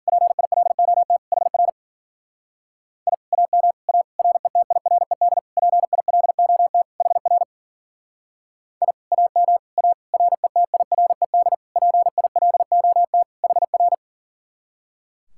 Morse code, heard as a keyed tone, about 52 wpm, 700 Hz. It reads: PILOT HR I AM A RETIRED PILOT HR I AM A RETIRED PILOT HR